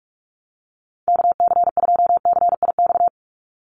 Morse code, heard as a keyed tone, about 44 wpm, 700 Hz. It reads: KC2CIX
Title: USA Call Signs 45wpm